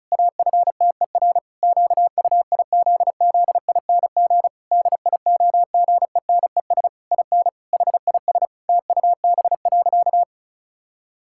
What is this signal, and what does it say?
Morse code, heard as a keyed tone, about 36 wpm, 700 Hz. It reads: AFTER QUIZZING DIOGENES IN HIS TUB.
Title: Pangram Challenge v2 - 35wpm